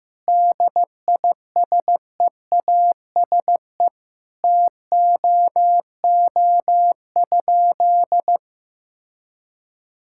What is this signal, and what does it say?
Morse code, heard as a keyed tone, about 15 wpm, 700 Hz. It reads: DISEASE TOO?
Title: Sentences from Top 1000 Words - Review Part 6 - 15wpm